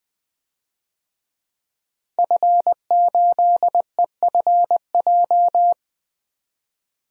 Morse code, heard as a keyed tone, about 20 wpm, 700 Hz. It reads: F8EFJ
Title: DXCC-100 Call Signs - 5 Characters - 20wpm